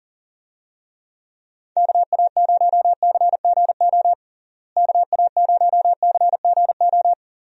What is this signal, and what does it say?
Morse code, heard as a keyed tone, about 40 wpm, 700 Hz. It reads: KA0CGO KA0CGO
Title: USA Call Signs - 6 Characters - Repeated 3x - 40wpm